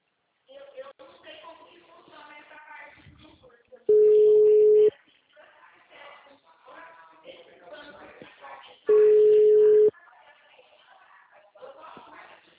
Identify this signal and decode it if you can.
Ringback tone (Europe, 425 Hz), with speech.